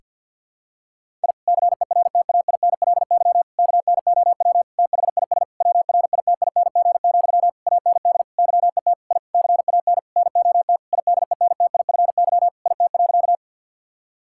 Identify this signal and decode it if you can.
Morse code, heard as a keyed tone, about 50 wpm, 700 Hz. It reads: I CERTAINLY KNOW THIS WRITING, AND YET I CAN NOT IDENTIFY IT.
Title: Sentences from Top 1000 Words - Review Part 6 - 50wpm